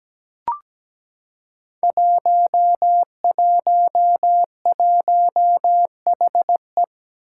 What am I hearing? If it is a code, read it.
Morse code, heard as a keyed tone, about 17 wpm, 700 Hz. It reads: 111HE
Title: Five Challenging Letter-Number Combinations 17wpm